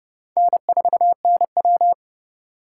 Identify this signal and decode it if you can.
Morse code, heard as a keyed tone, about 30 wpm, 700 Hz. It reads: N4NW
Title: K1USN SST QSO Practice - 30wpm